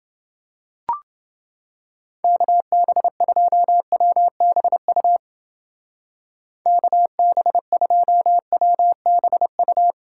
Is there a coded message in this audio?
Yes: Morse code from 2.2 to 9.9 s, heard as a keyed tone, about 28 wpm, 700 Hz. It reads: KB2WBU KB2WBU